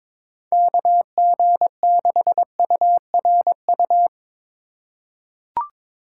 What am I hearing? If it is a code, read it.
Morse code, heard as a keyed tone, about 23 wpm, 700 Hz. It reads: KG6URU
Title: USA Call Signs - 4x Character Spacing - 22wpm